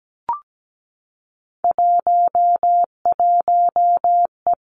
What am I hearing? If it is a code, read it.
Morse code, heard as a keyed tone, about 17 wpm, 700 Hz. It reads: 11E
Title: Three Challenging Letter-Number Combinations 17wpm